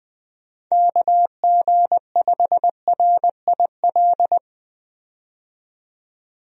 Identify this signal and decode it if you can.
Morse code, heard as a keyed tone, about 20 wpm, 700 Hz. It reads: KG5RIL
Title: USA Call Signs - 6 Characters - 20wpm